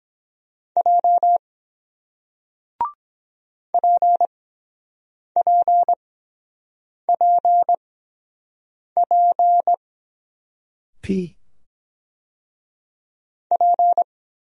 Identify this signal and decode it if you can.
Morse code, heard as a keyed tone, about 23 wpm, spaced as for 7 wpm, 700 Hz. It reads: J PPPP P